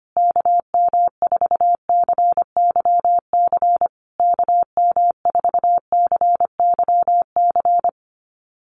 Morse code, heard as a keyed tone, about 25 wpm, 700 Hz. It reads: KM4CYC KM4CYC
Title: USA Call Signs Repeated 3x 25wpm